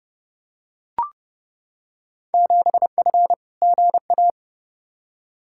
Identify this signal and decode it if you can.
Morse code, heard as a keyed tone, about 31 wpm, 700 Hz. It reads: 7F GA
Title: Field Day Exchanges Encore - 30wpm